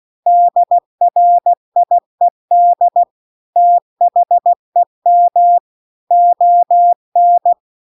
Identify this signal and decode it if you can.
Morse code, heard as a keyed tone, about 16 wpm, 700 Hz. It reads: DRIED THEM ON
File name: CW-The-Big-Twoheaded-River-2-16WPM